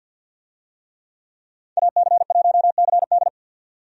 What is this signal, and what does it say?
Morse code, heard as a keyed tone, about 50 wpm, 700 Hz. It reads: AC1CD
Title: USA Call Signs - 5 Characters - 50wpm